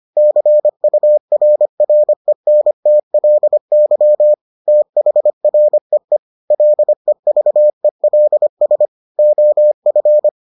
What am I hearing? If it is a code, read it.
Morse code, heard as a keyed tone, about 25 wpm, 600 Hz. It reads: CURRENTLY THREE LEVELS OF